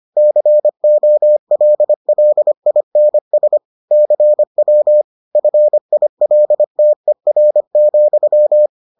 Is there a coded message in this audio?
Yes: Morse code from 0.2 to 8.7 s, heard as a keyed tone, about 25 wpm, 600 Hz. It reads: COLLINS CW FILTER,